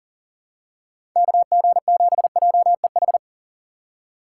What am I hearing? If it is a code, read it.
Morse code, heard as a keyed tone, about 40 wpm, 700 Hz. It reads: KG7JEH
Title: USA Call Signs - 6 Characters - 3x Character Spacing - 40wpm